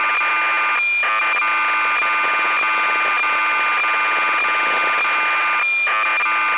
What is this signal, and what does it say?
Cassette data (Kansas City type).